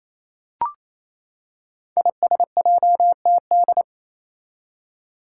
Morse code, heard as a keyed tone, about 28 wpm, 700 Hz. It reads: ISJTD
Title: Five Challenging Letter-Number Combinations 28wpm